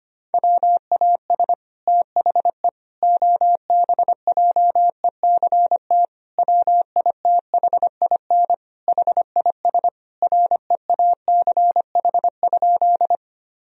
Morse code, heard as a keyed tone, about 25 wpm, 700 Hz. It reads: WAS THE OBJECT WITHIN HIS REACH?
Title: Sentences from Top 1000 Words - Review Part 4 - 25wpm